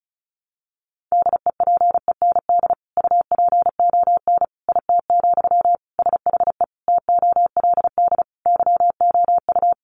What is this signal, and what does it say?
Morse code, heard as a keyed tone, about 34 wpm, 700 Hz. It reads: DEPEND UPON IT, SHE TOLD YOU
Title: Sentences from Top 1300 Words 35wpm